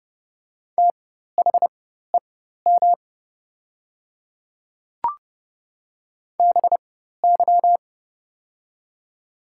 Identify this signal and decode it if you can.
Morse code, heard as a keyed tone, about 30 wpm, spaced as for 14 wpm, 690 Hz. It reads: THEM BY
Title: 4x Character Spacing - Top 100 Words - 30wpm